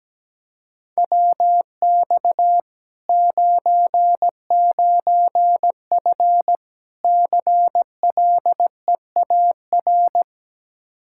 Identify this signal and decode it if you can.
Morse code, heard as a keyed tone, about 17 wpm, 700 Hz. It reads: WX 99F CLEAR
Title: Practice QSO - Exchange 2 - Info - 17wpm